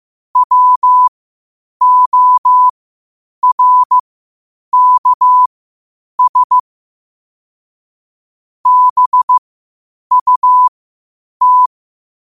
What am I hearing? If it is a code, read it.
Morse code, heard as a keyed tone, about 15 wpm, spaced as for 8 wpm, 1000 Hz. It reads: WORKS BUT
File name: qotdfiles / 8wpm